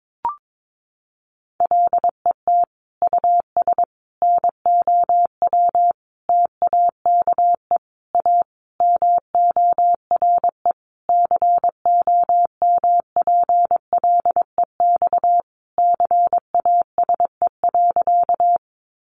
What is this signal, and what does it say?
Morse code, heard as a keyed tone, about 22 wpm, 700 Hz. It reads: LET US NOW TAKE A MORE COMPLEX CASE.
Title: Sentences from Top 900 Words 22wpm